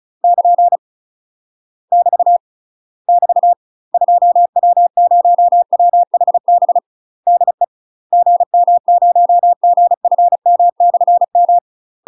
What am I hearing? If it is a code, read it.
Morse code, heard as a keyed tone, about 34 wpm, 700 Hz. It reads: <KN> <BT> <BT> 2W0WHB DE GM0GFM/M
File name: UK MASTER QSOs TRAINING / 35WPM / UK MASTER QSOs 0042